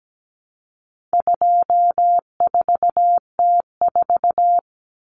Morse code, heard as a keyed tone, about 17 wpm, 700 Hz. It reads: 24T4
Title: Common Cut Numbers - 4-Digit 17wpm